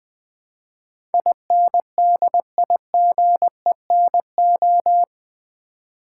Morse code, heard as a keyed tone, about 20 wpm, 700 Hz. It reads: INDIGENO
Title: Build A Word - 4 Letters At A Time - 10-Letter Words - 20wpm